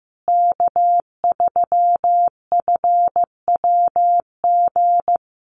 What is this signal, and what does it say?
Morse code, heard as a keyed tone, about 15 wpm, 700 Hz. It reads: K3FWG
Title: USA Call Signs Repeated 3x 15wpm